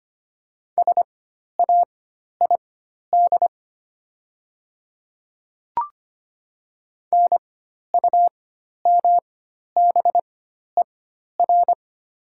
Morse code, heard as a keyed tone, about 25 wpm, spaced as for 12 wpm, 700 Hz. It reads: SAID NUMBER
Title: 4x Character Spacing - Top 100 Words - 25wpm